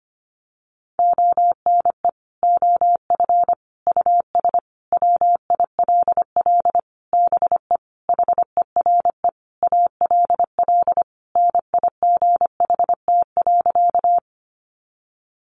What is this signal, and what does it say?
Morse code, heard as a keyed tone, about 25 wpm, 700 Hz. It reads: ONE OF US WILL BE HERE ALL NIGHT.
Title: Sentences from Top 300 Words 25wpm